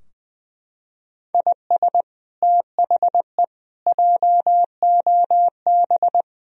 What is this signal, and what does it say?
Morse code, heard as a keyed tone, about 20 wpm, 700 Hz. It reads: IS THE JOB
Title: Sentences from Top 1000 Words - Review Part 3 - 20wpm